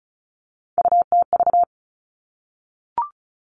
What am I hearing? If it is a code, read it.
Morse code, heard as a keyed tone, about 36 wpm, 700 Hz. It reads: UTV
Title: Three Challenging Letter-Number Combinations 35wpm